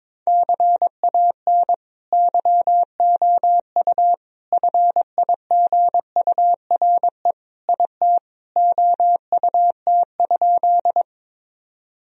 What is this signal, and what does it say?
Morse code, heard as a keyed tone, about 21 wpm, 700 Hz. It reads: CAN YOU FIGURE IT OUT?
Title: Sentences from Top 1000 Words - Review Part 2 - 22wpm